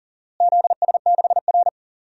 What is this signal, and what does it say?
Morse code, heard as a keyed tone, about 37 wpm, 700 Hz. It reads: ZS6R
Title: DXCC-100 Call Signs - 4 Characters - Repeated 2x - 40wpm